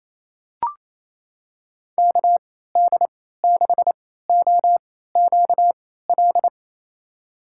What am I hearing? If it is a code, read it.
Morse code, heard as a keyed tone, about 28 wpm, spaced as for 16 wpm, 700 Hz. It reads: KD6OQL
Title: USA Call Signs - 6 Characters - 3x Character Spacing - 28wpm